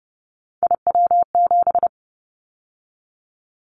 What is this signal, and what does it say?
Morse code, heard as a keyed tone, about 30 wpm, 700 Hz. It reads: IW7
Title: Callsign Prefixes for Top 25 DXCC - 30wpm